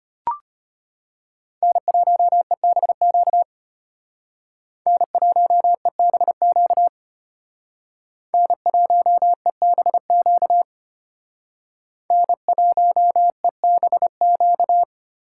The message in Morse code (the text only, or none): N1EBQ N1EBQ N1EBQ N1EBQ